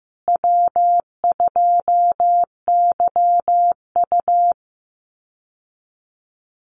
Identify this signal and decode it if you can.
Morse code, heard as a keyed tone, about 15 wpm, 700 Hz. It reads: W2YU
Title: USA Call Signs Repeated 3x 15wpm